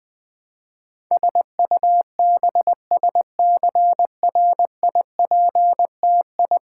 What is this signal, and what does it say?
Morse code, heard as a keyed tone, about 20 wpm, 700 Hz. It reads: SUBSCRIPTI
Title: Build A Word - 5 Letters At A Time - 12-Letter Words - 20wpm